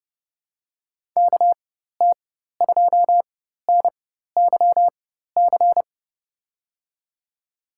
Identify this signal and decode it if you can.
Morse code, heard as a keyed tone, about 30 wpm, spaced as for 14 wpm, 700 Hz. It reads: KT2NYC
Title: USA Call Signs - 6 Characters - 4x Character Spacing - 30wpm